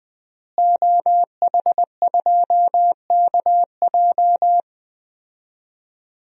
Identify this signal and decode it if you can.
Morse code, heard as a keyed tone, about 21 wpm, 700 Hz. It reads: OH2KJ
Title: DXCC-100 Call Signs - 5 Characters - Repeated 2x - 20wpm